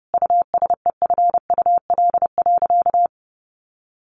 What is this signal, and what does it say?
Morse code, heard as a keyed tone, about 30 wpm, 700 Hz. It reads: USEFUL.
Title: Sentences from Top 1500 Words 30wpm